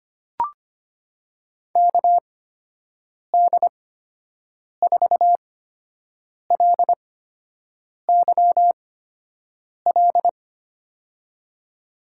Morse code, heard as a keyed tone, about 25 wpm, spaced as for 7 wpm, 700 Hz. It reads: KD4LYL
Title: USA Call Signs - 6 Characters - 8x Character Spacing - 25wpm